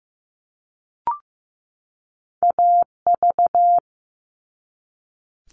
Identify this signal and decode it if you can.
Morse code, heard as a keyed tone, about 15 wpm, 700 Hz. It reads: AV
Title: All Cut Numbers - 2-Digit 15wpm